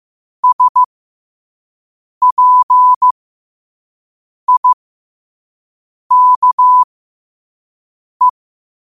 Morse code, heard as a keyed tone, about 15 wpm, spaced as for 5 wpm, 1000 Hz. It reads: SPIKE
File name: qotdfiles / 5wpm